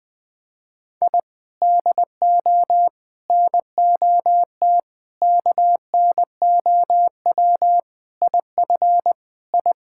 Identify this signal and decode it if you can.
Morse code, heard as a keyed tone, about 21 wpm, 700 Hz. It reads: I DO NOT KNOW IF I
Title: Sentences with 5-Word Phrases - 20wpm